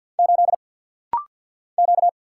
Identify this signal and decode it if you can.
Morse code, heard as a keyed tone, about 49 wpm, 700 Hz. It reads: 8 Q
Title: Single Letter-Number - Rapid-Fire 50wpm